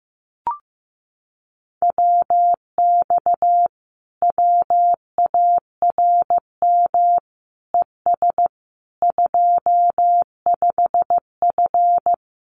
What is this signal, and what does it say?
Morse code, heard as a keyed tone, about 15 wpm, 700 Hz. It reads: WX WARM ES 25F
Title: Common QSO Phrases Encore 15wpm